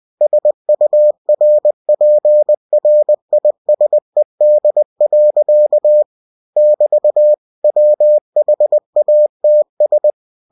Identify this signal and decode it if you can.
Morse code, heard as a keyed tone, about 20 wpm, 600 Hz. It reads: SURPRISED. <BT> WHATS
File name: cw-books / winnie-the-pooh / winnie-20wpm-page-0108